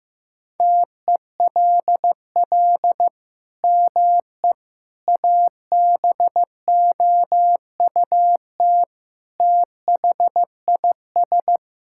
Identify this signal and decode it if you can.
Morse code, heard as a keyed tone, about 15 wpm, 700 Hz. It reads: TELL ME ABOUT THIS
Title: Sentences from Top 1000 Words - Review Part 6 - 15wpm